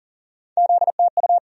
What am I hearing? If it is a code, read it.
Morse code, heard as a keyed tone, about 40 wpm, 700 Hz. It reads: ZTU
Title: USA Call Sign Postfixes - 3 Letters 40wpm